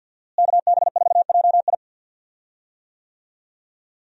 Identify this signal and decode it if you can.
Morse code, heard as a keyed tone, about 49 wpm, 700 Hz. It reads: KB4JI
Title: USA Call Signs - 5 Characters - Repeated 2x - 50wpm